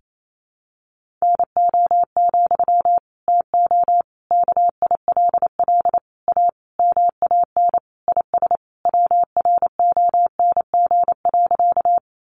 Morse code, heard as a keyed tone, about 28 wpm, 700 Hz. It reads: NO, TO KILL A MAN IS WRONG.
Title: Sentences from Top 1400 Words 28wpm